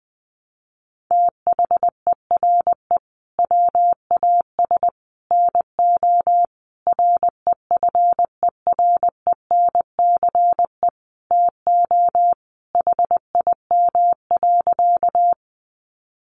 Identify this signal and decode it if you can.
Morse code, heard as a keyed tone, about 20 wpm, 700 Hz. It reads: THERE WAS NO REFERENCE TO HIM.
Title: Sentences from Top 1200 Words 20wpm